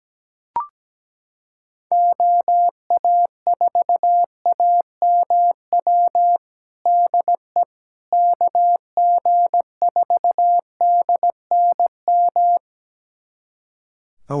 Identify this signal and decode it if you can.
Morse code, heard as a keyed tone, about 17 wpm, 700 Hz. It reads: OA4AMW DE KG4DNM
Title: USA Callsign DE Callsign 17wpm